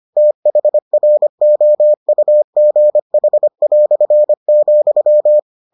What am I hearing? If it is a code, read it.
Morse code, heard as a keyed tone, about 25 wpm, 600 Hz. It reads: THROUGH",